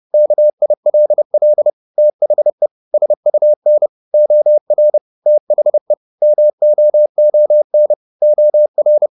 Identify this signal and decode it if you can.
Morse code, heard as a keyed tone, about 30 wpm, 600 Hz. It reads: KILL THE SUN OR THE MOON OR